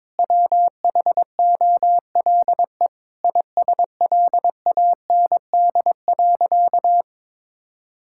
Morse code, heard as a keyed tone, about 22 wpm, 700 Hz. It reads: WHOLE ISLAND.
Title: Sentences from Top 1000 Words - Review Part 5 - 22wpm